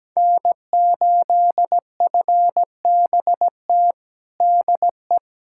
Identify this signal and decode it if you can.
Morse code, heard as a keyed tone, about 17 wpm, 700 Hz. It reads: N8FBT DE
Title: USA Callsign DE Callsign - 5 Characters - 17wpm